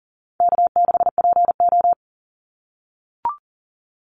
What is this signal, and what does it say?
Morse code, heard as a keyed tone, about 40 wpm, 700 Hz. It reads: K6PO
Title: Easy Call Signs - 40wpm